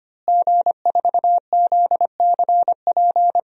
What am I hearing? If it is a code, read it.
Morse code, heard as a keyed tone, about 25 wpm, 700 Hz. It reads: G4ZCP
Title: DXCC-100 Call Signs - 5 Characters - Repeated 2x - 25wpm